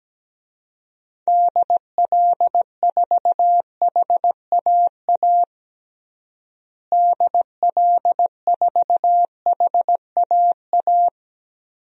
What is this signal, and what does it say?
Morse code, heard as a keyed tone, about 17 wpm, 700 Hz. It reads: DL4HAA DL4HAA
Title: DXCC-100 Call Signs - 6 Characters - Repeated 3x - 17wpm